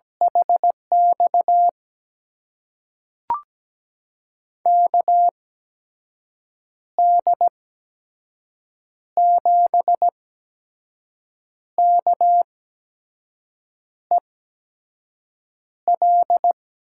Morse code, heard as a keyed tone, about 17 wpm, spaced as for 5 wpm, 700 Hz. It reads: HX KD7KEL
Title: USA Call Signs - 6 Characters - 8x Character Spacing - 17wpm